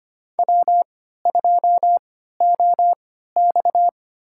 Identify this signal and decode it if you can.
Morse code, heard as a keyed tone, about 25 wpm, spaced as for 14 wpm, 700 Hz. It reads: W2OX
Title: USA Call Signs - 3x Character Spacing - 25wpm